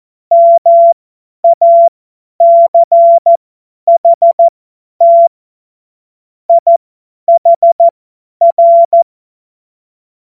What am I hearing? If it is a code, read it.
Morse code, heard as a keyed tone, about 14 wpm, spaced as for 10 wpm, 680 Hz. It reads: MACHT IHR
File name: Morsekurs / Witze / Witz 21. September 2021, 50 F70